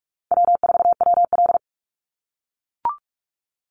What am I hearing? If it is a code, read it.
Morse code, heard as a keyed tone, about 44 wpm, 700 Hz. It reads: W4WL
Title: USA Call Signs 45wpm